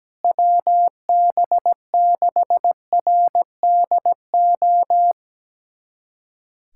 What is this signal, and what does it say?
Morse code, heard as a keyed tone, about 17 wpm, 700 Hz. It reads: WB6RDO
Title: USA Callsign DE Callsign 17wpm